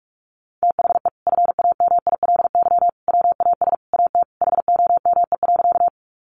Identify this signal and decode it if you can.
Morse code, heard as a keyed tone, about 46 wpm, 700 Hz. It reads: THE FAMILY WAS AT HOME.
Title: Sentences from Top 400 Words 45wpm